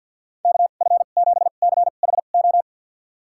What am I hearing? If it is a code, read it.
Morse code, heard as a keyed tone, about 50 wpm, spaced as for 36 wpm, 700 Hz. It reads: KF7CHO
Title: USA Call Signs - 2x Character Spacing - 50wpm